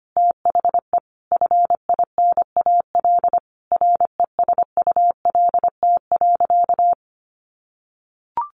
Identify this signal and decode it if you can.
Morse code, heard as a keyed tone, about 25 wpm, 700 Hz. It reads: THE FINAL RESULT.
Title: Sentences from Top 800 Words 25wpm